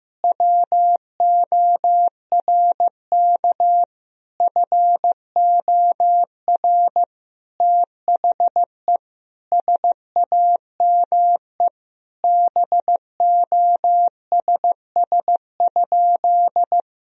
Morse code, heard as a keyed tone, about 15 wpm, 700 Hz. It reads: WORK FOR THE SAME BOSS?
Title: Sentences from Top 1000 Words - Review Part 4 - 15wpm